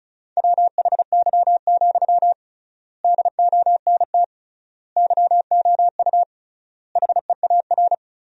Morse code, heard as a keyed tone, about 34 wpm, 700 Hz. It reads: WHY, DONT YOU HEAR
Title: 3x Word-Spacing - Sentences from Top 300 Words - 35wpm